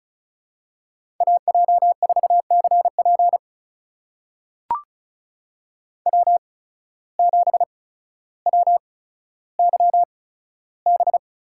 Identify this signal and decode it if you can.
Morse code, heard as a keyed tone, about 36 wpm, spaced as for 15 wpm, 700 Hz. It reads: AJ4CP W7WYB
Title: USA Call Signs - 5 Characters - 8x Character Spacing - 35wpm